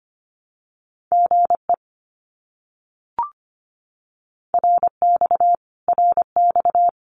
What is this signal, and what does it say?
Morse code, heard as a keyed tone, about 25 wpm, 700 Hz. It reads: GE RX RX
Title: Abbreviations Repeated 3x - 25wpm